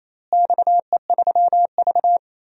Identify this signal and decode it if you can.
Morse code, heard as a keyed tone, about 28 wpm, 700 Hz. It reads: XE3V
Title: DXCC-100 Call Signs - 4 Characters - Repeated 2x - 28wpm